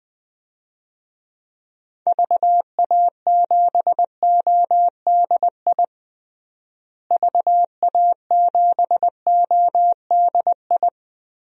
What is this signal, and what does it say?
Morse code, heard as a keyed tone, about 20 wpm, 700 Hz. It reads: VA7ODI VA7ODI
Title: DXCC-100 Call Signs - 6 Characters - Repeated 2x - 20wpm